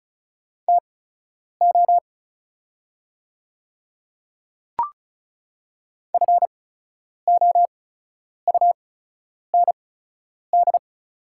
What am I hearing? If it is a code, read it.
Morse code, heard as a keyed tone, about 36 wpm, spaced as for 10 wpm, 700 Hz. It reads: TO FOUND
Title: 8x Character Spacing - Top 300 Words - 35wpm